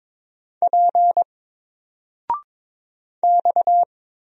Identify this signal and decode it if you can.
Morse code, heard as a keyed tone, about 22 wpm, 700 Hz. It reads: P X
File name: Single-Letter-Number-22wpm